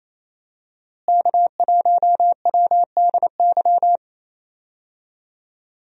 Morse code, heard as a keyed tone, about 28 wpm, 700 Hz. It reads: K1WDY
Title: USA Call Signs - 5 Characters - Repeated 2x - 28wpm